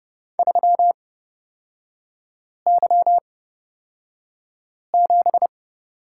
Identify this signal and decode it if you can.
Morse code, heard as a keyed tone, about 30 wpm, spaced as for 5 wpm, 700 Hz. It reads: 3Y7
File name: Lesson 36e - BT - Three Characters 30wpm